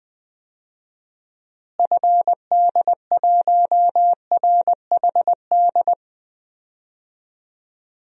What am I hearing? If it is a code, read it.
Morse code, heard as a keyed tone, about 20 wpm, 700 Hz. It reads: FD1RHD
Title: DXCC-100 Call Signs - 6 Characters - Repeated 2x - 20wpm